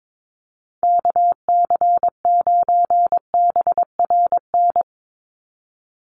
Morse code, heard as a keyed tone, about 22 wpm, 700 Hz. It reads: KC9BRN
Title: USA Call Signs 22wpm